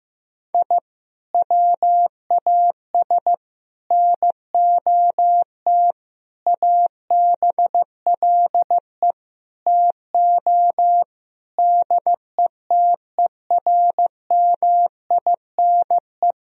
Morse code, heard as a keyed tone, about 15 wpm, 700 Hz. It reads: I WAS NOT ABLE TO DETERMINE
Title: Sentences from Top 1000 Words - Review Part 4 - 15wpm